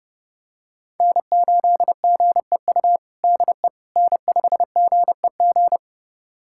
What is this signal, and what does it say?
Morse code, heard as a keyed tone, about 30 wpm, 700 Hz. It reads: N8GEU DE N5GEG
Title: USA Callsign DE Callsign - 5 Characters - 30wpm